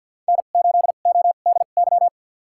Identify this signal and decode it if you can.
Morse code, heard as a keyed tone, about 48 wpm, spaced as for 35 wpm, 700 Hz. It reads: N8ODY
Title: USA Call Signs - 5 Characters - 2x Character Spacing - 50wpm